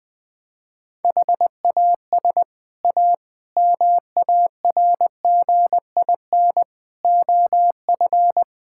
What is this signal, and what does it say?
Morse code, heard as a keyed tone, about 20 wpm, 700 Hz. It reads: HAS A MARGIN OF
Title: Sets of 6 Words - 20wpm